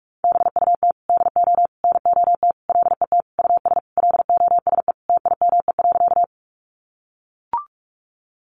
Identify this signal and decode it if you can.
Morse code, heard as a keyed tone, about 44 wpm, 700 Hz. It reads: BUT DO NOT LET US LOSE TIME.
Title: Sentences from Top 1100 Words 45wpm